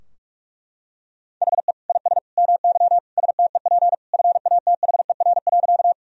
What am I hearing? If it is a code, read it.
Morse code, heard as a keyed tone, about 41 wpm, 700 Hz. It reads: HE IS MY STEP FATHER.
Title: Sentences from Top 1000 Words - Review Part 2 - 45wpm